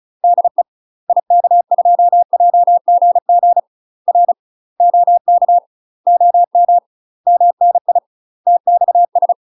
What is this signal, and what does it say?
Morse code, heard as a keyed tone, about 36 wpm, 700 Hz. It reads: DE IK2JGG R OK OM MNI TXS
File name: UK MASTER QSOs TRAINING / 35WPM / UK MASTER QSOs 0023